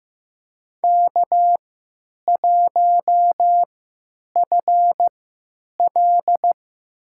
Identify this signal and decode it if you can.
Morse code, heard as a keyed tone, about 15 wpm, spaced as for 8 wpm, 700 Hz. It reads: K1FL
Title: USA Call Signs - 4 Characters - 3x Character Spacing - 15wpm